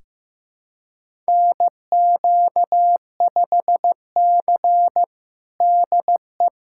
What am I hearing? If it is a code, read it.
Morse code, heard as a keyed tone, about 15 wpm, 700 Hz. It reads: NQ5C DE
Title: USA Callsign DE Callsign - 4 Characters - 15wpm